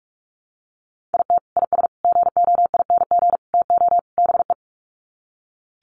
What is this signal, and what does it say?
Morse code, heard as a keyed tone, about 44 wpm, 700 Hz. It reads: IT IS GOING TO BE
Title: Sets of 5 Words 45wpm